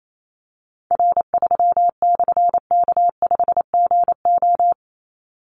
Morse code, heard as a keyed tone, about 28 wpm, 700 Hz. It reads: R3/K5GO
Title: Call Signs - 28wpm